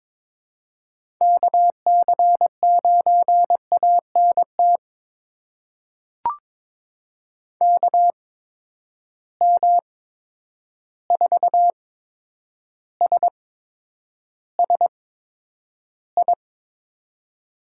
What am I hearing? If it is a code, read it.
Morse code, heard as a keyed tone, about 22 wpm, spaced as for 9 wpm, 700 Hz. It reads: KC9ANT KM4SSI
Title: USA Call Signs - 6 Characters - 8x Character Spacing - 22wpm